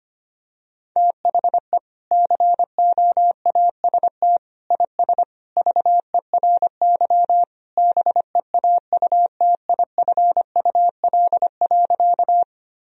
Morse code, heard as a keyed tone, about 25 wpm, 700 Hz. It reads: THE COAST IS VERY BEAUTIFUL.
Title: Sentences from Top 1000 Words - Review Part 4 - 25wpm